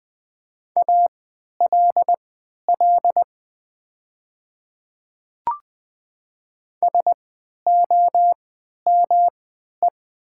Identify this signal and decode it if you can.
Morse code, heard as a keyed tone, about 20 wpm, spaced as for 11 wpm, 700 Hz. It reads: ALL SOME
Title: 3x Character Spacing - Top 500 Words - 20wpm